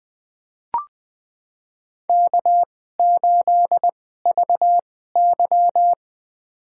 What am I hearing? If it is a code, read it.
Morse code, heard as a keyed tone, about 20 wpm, spaced as for 14 wpm, 700 Hz. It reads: K8VY
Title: USA Call Signs - 4 Characters - 2x Character Spacing - 20wpm